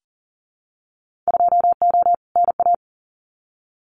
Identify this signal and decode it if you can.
Morse code, heard as a keyed tone, about 38 wpm, 700 Hz. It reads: 2O NA
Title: Winter Field Day Exchanges 40wpm